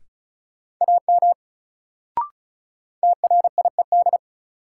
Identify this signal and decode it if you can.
Morse code, heard as a keyed tone, about 34 wpm, 700 Hz. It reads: AM TRIED
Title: Sets of 2 Words - New Word Drill - 35wpm